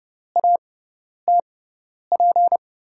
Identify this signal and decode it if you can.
Morse code, heard as a keyed tone, about 30 wpm, spaced as for 10 wpm, 700 Hz. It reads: ATP
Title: USA Call Signs - 6 Characters - 6x Character Spacing - 30wpm